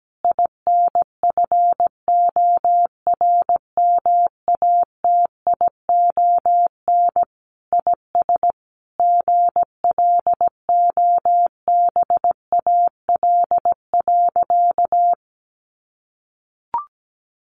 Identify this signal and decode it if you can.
Morse code, heard as a keyed tone, about 17 wpm, 700 Hz. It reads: INFORMATION IS GLOBAL.
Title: Sentences from Top 1200 Words 17wpm